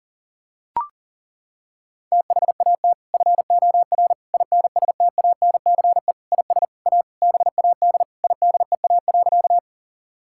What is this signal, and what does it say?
Morse code, heard as a keyed tone, about 40 wpm, 700 Hz. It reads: THAT FOR INSTANCE IS A BAD IDEA.
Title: Sentences from Top 1000 Words - Review Part 5 - 40wpm